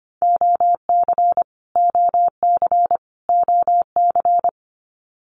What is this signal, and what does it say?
Morse code, heard as a keyed tone, about 25 wpm, 700 Hz. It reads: OC OC OC
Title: Abbreviations Repeated 3x - 25wpm